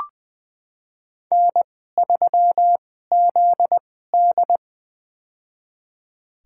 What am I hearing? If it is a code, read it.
Morse code, heard as a keyed tone, about 21 wpm, spaced as for 15 wpm, 700 Hz. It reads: N3ZD
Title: USA Call Signs - 4 Characters - 2x Character Spacing - 20wpm